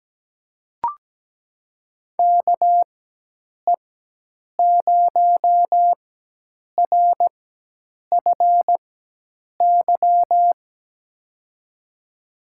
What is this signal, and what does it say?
Morse code, heard as a keyed tone, about 17 wpm, spaced as for 8 wpm, 700 Hz. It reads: KE0RFY
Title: USA Call Signs - 4x Character Spacing - 17wpm